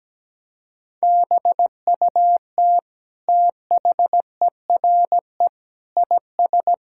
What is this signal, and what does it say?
Morse code, heard as a keyed tone, about 17 wpm, 700 Hz. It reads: BUT THERE IS
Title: Sentences from Top 1000 Words - Review Part 6 - 17wpm